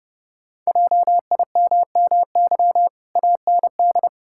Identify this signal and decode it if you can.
Morse code, heard as a keyed tone, about 30 wpm, 700 Hz. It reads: JIMMY AND
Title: Pangram Challenge v2 - 30wpm